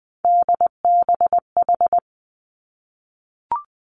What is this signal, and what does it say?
Morse code, heard as a keyed tone, about 20 wpm, 700 Hz. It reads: DBH
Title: Three Challenging Letter-Number Combinations 20wpm